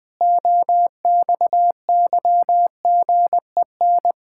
Morse code, heard as a keyed tone, about 19 wpm, 700 Hz. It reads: OXYGEN
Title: Pangram Challenge v2 - 20wpm